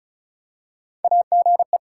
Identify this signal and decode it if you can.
Morse code, heard as a keyed tone, about 34 wpm, 700 Hz. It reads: AGE
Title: Practice QSO - Exchange 3 - More Info - 35wpm